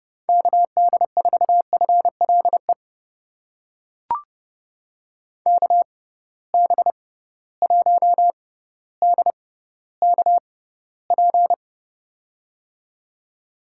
Morse code, heard as a keyed tone, about 31 wpm, spaced as for 16 wpm, 700 Hz. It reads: KD4FLE KB1DKP
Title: USA Call Signs - 6x Character Spacing - 30wpm